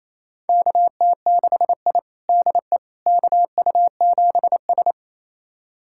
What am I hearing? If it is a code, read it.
Morse code, heard as a keyed tone, about 28 wpm, 700 Hz. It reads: KT6I DE KU7S
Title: USA Callsign DE Callsign - 4 Characters - 28wpm